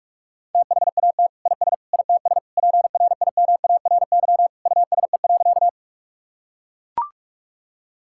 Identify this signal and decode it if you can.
Morse code, heard as a keyed tone, about 42 wpm, 700 Hz. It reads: THAT IS ITS PRIMARY USE.
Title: Sentences from Top 1000 Words - Review Part 5 - 45wpm